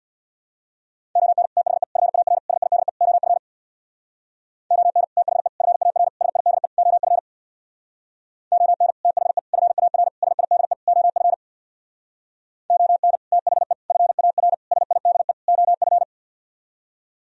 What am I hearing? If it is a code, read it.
Morse code, heard as a keyed tone, about 58 wpm, 700 Hz. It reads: ON THE FAR SIDE OF ON THE FAR SIDE OF ON THE FAR SIDE OF ON THE FAR SIDE OF
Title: Speed-Racing - Sets of 5 Words 50wpm